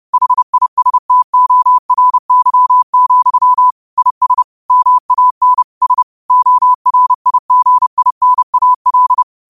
Morse code, heard as a keyed tone, about 30 wpm, 1000 Hz. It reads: HISTORY, IS MAN S ORIGINAL
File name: qotdfiles / 30wpm